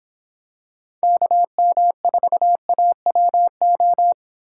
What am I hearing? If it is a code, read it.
Morse code, heard as a keyed tone, about 25 wpm, 700 Hz. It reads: KM4AWO